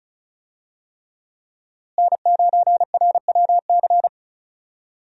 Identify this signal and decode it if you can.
Morse code, heard as a keyed tone, about 36 wpm, 700 Hz. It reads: N9RWC
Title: USA Call Signs - 5 Characters - Repeated 3x - 35wpm